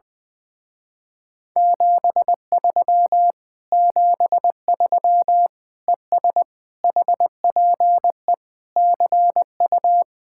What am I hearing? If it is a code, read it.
Morse code, heard as a keyed tone, about 21 wpm, 700 Hz. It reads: 73 73 ES HPE CU
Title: Speed-Racing Common QSO Phrases Encore - 20wpm